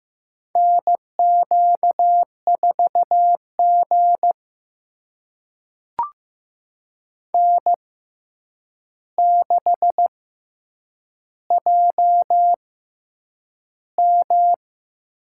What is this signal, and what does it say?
Morse code, heard as a keyed tone, about 15 wpm, spaced as for 8 wpm, 700 Hz. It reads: NQ4G N6JM
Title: USA Call Signs - 4 Characters - 6x Character Spacing - 15wpm